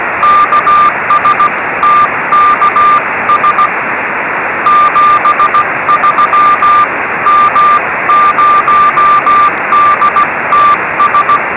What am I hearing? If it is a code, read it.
Morse code, heard as a keyed tone, about 16 wpm, 1190 Hz, with radio noise. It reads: KS TKS 73 M0DTS
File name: 3cm / g1gey cw